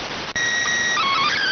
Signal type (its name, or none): SSTV (slow-scan television)